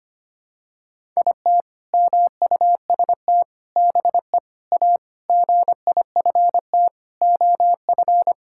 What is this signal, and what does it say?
Morse code, heard as a keyed tone, about 25 wpm, 700 Hz. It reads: IT MUST BE A GIFT OF
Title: Sentences from Top 1000 Words - Review Part 6 - 25wpm